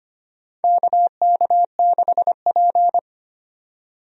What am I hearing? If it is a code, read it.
Morse code, heard as a keyed tone, about 25 wpm, 700 Hz. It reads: KK6P
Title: USA Call Signs - 4 Characters - 25wpm